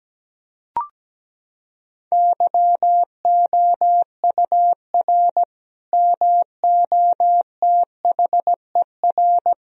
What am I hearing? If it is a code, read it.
Morse code, heard as a keyed tone, about 17 wpm, 700 Hz. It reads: YOUR MOTHER
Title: Sentences from Top 1000 Words - Review Part 2 - 17wpm